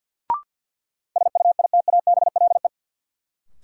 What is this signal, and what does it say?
Morse code, heard as a keyed tone, about 53 wpm, 700 Hz. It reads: SUITABLE
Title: Sentences from Top 1400 Words - New Word Drill Alt - 50wpm